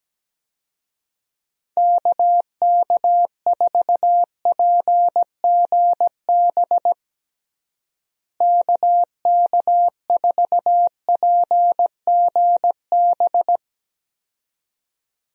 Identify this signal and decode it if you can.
Morse code, heard as a keyed tone, about 17 wpm, 700 Hz. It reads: KK4PGB KK4PGB
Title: USA Call Signs - 6 Characters - Repeated 2x - 17wpm